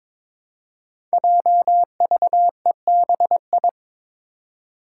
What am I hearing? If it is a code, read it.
Morse code, heard as a keyed tone, about 22 wpm, 700 Hz. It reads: JVEBI
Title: Five Challenging Letter-Number Combinations 22wpm